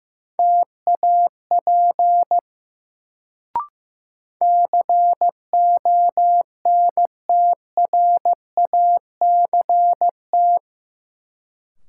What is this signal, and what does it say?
Morse code, heard as a keyed tone, about 16 wpm, 700 Hz. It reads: TAP CONTRACT
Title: Sentences from Top 1400 Words - New Word Drill - 15wpm